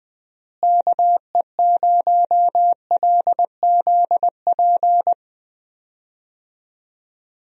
Morse code, heard as a keyed tone, about 20 wpm, 700 Hz. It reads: KE0LZP
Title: USA Call Signs - 6 Characters - Repeated 3x - 20wpm